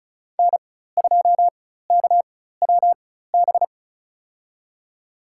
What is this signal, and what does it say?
Morse code, heard as a keyed tone, about 36 wpm, spaced as for 16 wpm, 700 Hz. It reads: N2KWB
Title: USA Call Signs - 5 Characters - 4x Character Spacing - 35wpm